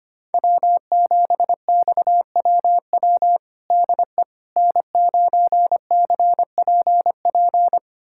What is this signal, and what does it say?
Morse code, heard as a keyed tone, about 25 wpm, 700 Hz. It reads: W7XWW DE N9CPP
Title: USA Callsign DE Callsign - 5 Characters - 25wpm